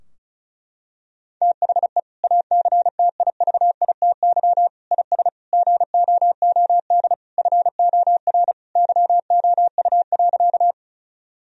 Morse code, heard as a keyed tone, about 34 wpm, 700 Hz. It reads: THE ACTIVITY IS GOOD FOR YOU.
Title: Sentences from Top 1000 Words - Review Part 4 - 35wpm